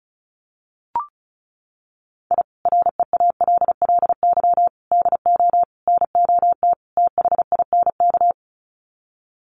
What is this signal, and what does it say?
Morse code, heard as a keyed tone, about 34 wpm, 700 Hz. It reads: I REALLY DO NOT THINK
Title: Sets of 5 Words Encore2 35wpm